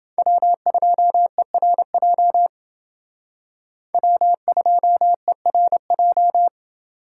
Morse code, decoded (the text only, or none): W2ERJ W2ERJ